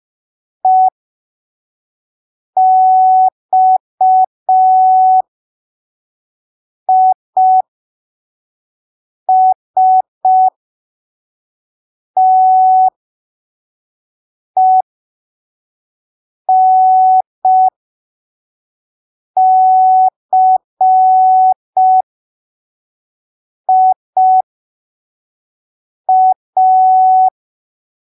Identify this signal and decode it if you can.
Morse code, heard as a keyed tone, about 5 wpm, spaced as for 3 wpm, 750 Hz. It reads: EXISTENCIA